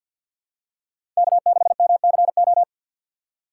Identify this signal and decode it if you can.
Morse code, heard as a keyed tone, about 50 wpm, 700 Hz. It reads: K6MCO
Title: USA Call Signs - 5 Characters - 4x Character Spacing - 50wpm